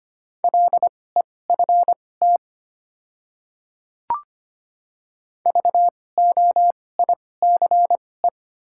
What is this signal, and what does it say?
Morse code, heard as a keyed tone, about 25 wpm, spaced as for 18 wpm, 700 Hz. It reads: LEFT VOICE
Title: 2x Character Spacing - Top 500 Words - 25wpm